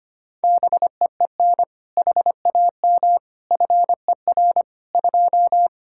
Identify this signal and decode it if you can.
Morse code, heard as a keyed tone, about 25 wpm, 700 Hz. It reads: BEEN HAM FER 2
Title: K1USN SST QSO Practice 25wpm